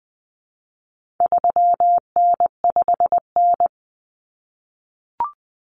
Morse code, heard as a keyed tone, about 20 wpm, 700 Hz. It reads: 3N5N
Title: Common Cut Numbers - 4-Digit 20wpm